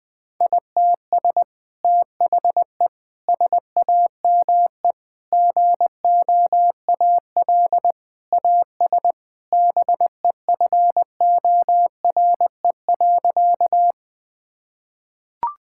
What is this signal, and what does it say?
Morse code, heard as a keyed tone, about 21 wpm, 700 Hz. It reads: ITS THE SAME GOAL AS BEFORE.
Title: Sentences from Top 1000 Words - Review Part 5 - 20wpm